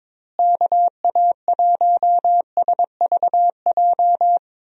Morse code, heard as a keyed tone, about 22 wpm, 700 Hz. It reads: KA1SVJ
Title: USA Call Signs - 6 Characters - Repeated 2x - 22wpm